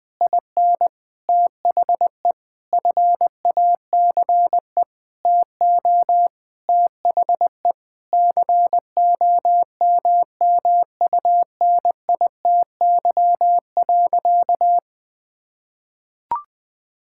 Morse code, heard as a keyed tone, about 20 wpm, 700 Hz. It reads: IN THE FACE TO THE COMMUNITY.
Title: Sentences with 8-Word Phrases - 20wpm